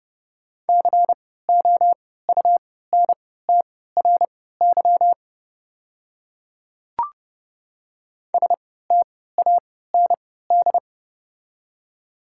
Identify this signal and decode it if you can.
Morse code, heard as a keyed tone, about 30 wpm, spaced as for 17 wpm, 700 Hz. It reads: COUNTRY STAND
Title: 3x Character Spacing - Top 300 Words - 30wpm